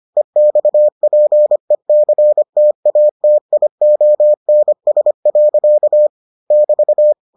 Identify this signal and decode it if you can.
Morse code, heard as a keyed tone, about 25 wpm, 600 Hz. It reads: EXPECTATIONS. <BT>